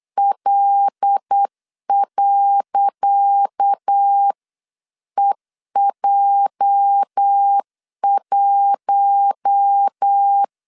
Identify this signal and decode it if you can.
Morse code, heard as a keyed tone, about 8 wpm, 800 Hz. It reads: L.EJ1
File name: LEZIONI / 10 SIGNS / 10C DOT 8wpm 3649